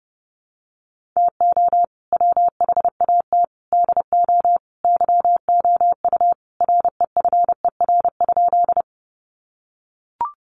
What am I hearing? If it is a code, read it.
Morse code, heard as a keyed tone, about 29 wpm, 700 Hz. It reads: TO WHAT DO YOU REFER?
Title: Sentences from Top 1000 Words 30wpm